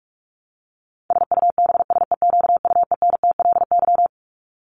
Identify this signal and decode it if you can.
Morse code, heard as a keyed tone, about 44 wpm, 700 Hz. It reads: SUBSEQUENTLY
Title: LY Suffix - Top 100 Words with Base Words 45wpm